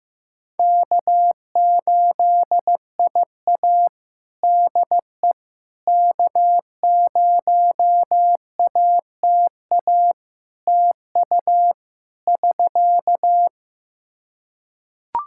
Morse code, heard as a keyed tone, about 15 wpm, 700 Hz. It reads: K8IA DE K0ATA TU <SK>
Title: Practice QSO - Exchange 4 - Ending - 15wpm